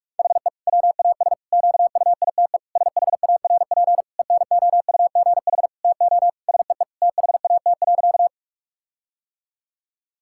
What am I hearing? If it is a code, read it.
Morse code, heard as a keyed tone, about 46 wpm, 700 Hz. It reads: HE WAS QUITE SHARP ENOUGH TO SEE THAT.
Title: Sentences from Top 1600 Words 45wpm